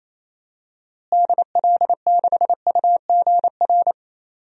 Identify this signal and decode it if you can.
Morse code, heard as a keyed tone, about 28 wpm, 700 Hz. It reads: DL6UGR
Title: DXCC-100 Call Signs - 6 Characters - Repeated 3x - 28wpm